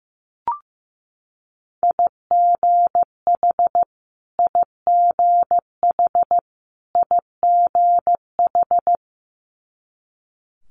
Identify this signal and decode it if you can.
Morse code, heard as a keyed tone, about 15 wpm, 700 Hz. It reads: IGH IGH IGH
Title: Multi-letter Phonemes Repeated 3x 15wpm